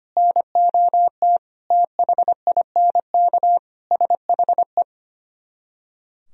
Morse code, heard as a keyed tone, about 33 wpm, 700 Hz. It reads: NOT THINK SHE